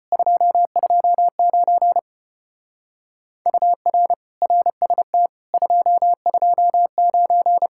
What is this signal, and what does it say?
Morse code, heard as a keyed tone, about 33 wpm, 700 Hz. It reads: 229 UR RST 229